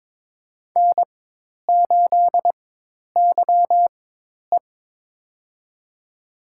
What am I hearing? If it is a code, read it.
Morse code, heard as a keyed tone, about 23 wpm, spaced as for 10 wpm, 700 Hz. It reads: N8YE
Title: USA Call Signs - 4x Character Spacing - 22wpm